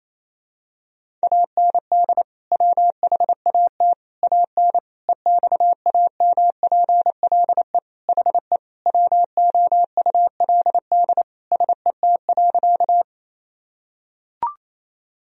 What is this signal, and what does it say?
Morse code, heard as a keyed tone, about 28 wpm, 700 Hz. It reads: AND WHAT AN EXAMPLE HE WOULD SET.
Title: Sentences from Top 1000 Words - Review Part 1 - 28wpm